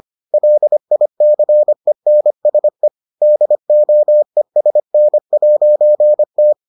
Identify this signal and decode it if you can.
Morse code, heard as a keyed tone, about 25 wpm, 600 Hz. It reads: LICENSE DOESN'T